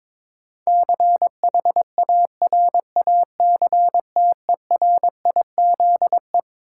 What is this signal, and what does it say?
Morse code, heard as a keyed tone, about 22 wpm, 700 Hz. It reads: CHARACTERIZE
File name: Twelve Letter Words - 22wpm